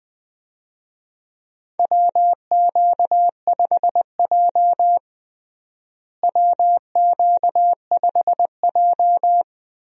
Morse code, heard as a keyed tone, about 21 wpm, 700 Hz. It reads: WQ5J WQ5J
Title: USA Call Signs - 4 Characters - Repeated 2x - 20wpm